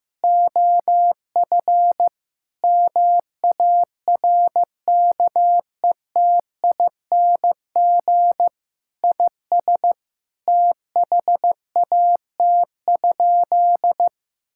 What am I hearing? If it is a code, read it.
Morse code, heard as a keyed tone, about 15 wpm, 700 Hz. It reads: OF MARKETING IS THAT?
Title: Sentences from Top 1000 Words - Review Part 5 - 15wpm